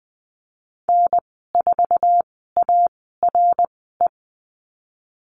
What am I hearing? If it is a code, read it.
Morse code, heard as a keyed tone, about 20 wpm, spaced as for 14 wpm, 700 Hz. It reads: N4ARE
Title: 062 Lesson 11g - Letter h - Call Signs 20wpm